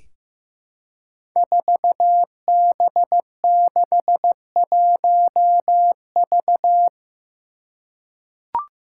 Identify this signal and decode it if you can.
Morse code, heard as a keyed tone, about 15 wpm, 700 Hz. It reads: 4B61V
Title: Five Challenging Letter-Number Combinations 15wpm